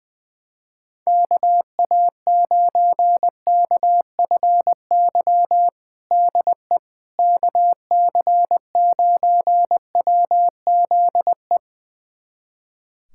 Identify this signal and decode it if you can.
Morse code, heard as a keyed tone, about 20 wpm, 700 Hz. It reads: KA9KFY DE KC9WZE
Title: USA Callsign DE Callsign - 6 Characters - 20wpm